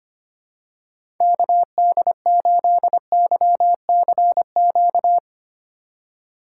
Morse code, heard as a keyed tone, about 25 wpm, 700 Hz. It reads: KD8YCQ
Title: USA Call Signs - 3x Character Spacing - 25wpm